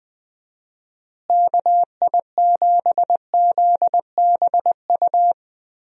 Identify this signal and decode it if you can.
Morse code, heard as a keyed tone, about 20 wpm, 700 Hz. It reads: KI7ZBU
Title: USA Call Signs - 6 Characters - Repeated 3x - 20wpm